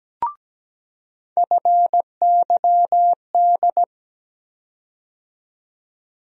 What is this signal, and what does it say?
Morse code, heard as a keyed tone, about 17 wpm, 700 Hz. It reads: FYD
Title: USA Call Sign Postfixes - 3 Letters 17wpm